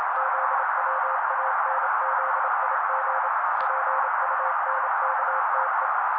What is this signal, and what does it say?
Morse code, heard as a keyed tone, about 27 wpm, 530 Hz. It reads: ZWANZIG PUNKTE
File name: daily / 27wpm noise